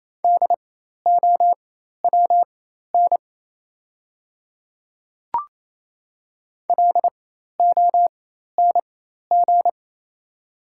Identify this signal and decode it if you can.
Morse code, heard as a keyed tone, about 27 wpm, spaced as for 13 wpm, 690 Hz. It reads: DOWN LONG
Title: 4x Character Spacing - Top 100 Words - 28wpm